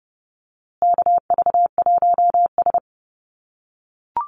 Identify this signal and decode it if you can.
Morse code, heard as a keyed tone, about 30 wpm, 700 Hz. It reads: KV1S
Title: USA Call Signs Repeated 2x 30wpm